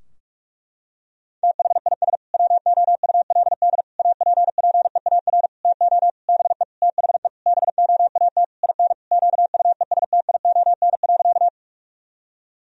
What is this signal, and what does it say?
Morse code, heard as a keyed tone, about 42 wpm, 700 Hz. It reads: THIS WOULD APPEAR TO BE THE BOAT IN QUESTION.
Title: Sentences from Top 1000 Words - Review Part 3 - 45wpm